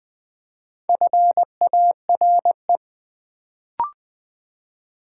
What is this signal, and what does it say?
Morse code, heard as a keyed tone, about 20 wpm, 700 Hz. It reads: FARE
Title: Build A Word - 2 Letters At A Time - 4-Letter Words - 20wpm